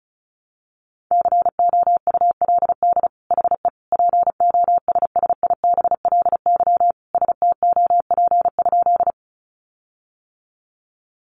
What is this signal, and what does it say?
Morse code, heard as a keyed tone, about 34 wpm, 700 Hz. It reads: COULD HE POSSIBLY STOP?
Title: Sentences from Top 1300 Words 35wpm